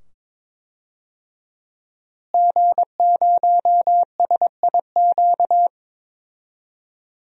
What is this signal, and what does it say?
Morse code, heard as a keyed tone, about 22 wpm, 700 Hz. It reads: G0SIQ
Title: DXCC-100 Call Signs - 5 Characters - Repeated 2x - 22wpm